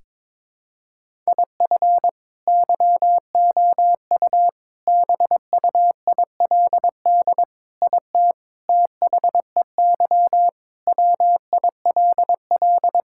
Morse code, heard as a keyed tone, about 22 wpm, 700 Hz. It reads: IF YOU BUILD IT THEY WILL
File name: Sentences with 7-Word Phrases - 22wpm